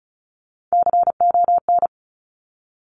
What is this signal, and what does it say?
Morse code, heard as a keyed tone, about 36 wpm, 700 Hz. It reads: CON
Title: Top Three Letter Combinations in Words 35wpm